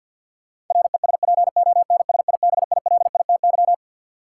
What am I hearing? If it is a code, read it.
Morse code, heard as a keyed tone, about 48 wpm, 700 Hz. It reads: RESPONSIBILITY
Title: Sentences from Top 1200 Words - New Word Drill Alt - 50wpm